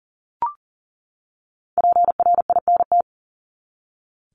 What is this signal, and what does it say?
Morse code, heard as a keyed tone, about 40 wpm, 700 Hz. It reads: PRINT
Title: Top 1251-1500 Words 40wpm